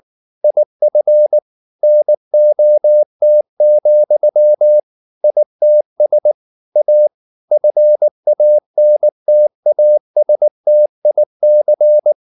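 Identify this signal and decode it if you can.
Morse code, heard as a keyed tone, about 19 wpm, 600 Hz. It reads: IF NOT, ITS A FANTASTIC
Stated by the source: Morse code at 19 wpm from CW med Gnister, reading Ken Follett's novel Never in English.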